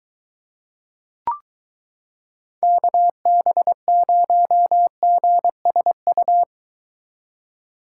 Morse code, heard as a keyed tone, about 22 wpm, 700 Hz. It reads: KB0GSU